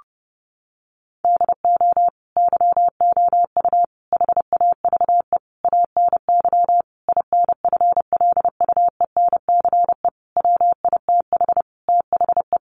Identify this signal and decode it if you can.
Morse code, heard as a keyed tone, about 30 wpm, 700 Hz. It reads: DO YOU HAVE ANY INFLUENCE WITH THE
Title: Sentences from Top 1100 Words 30wpm